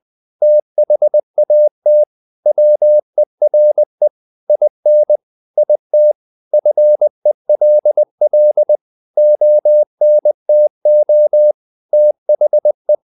Morse code, heard as a keyed tone, about 20 wpm, 600 Hz. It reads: THAT WERE IN IT FELL ONTO THE
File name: the old man and the sea part 2 20wpm